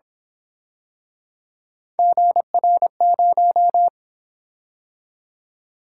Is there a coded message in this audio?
Yes: Morse code from 2.0 to 3.9 s, heard as a keyed tone, about 28 wpm, 700 Hz. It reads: GR0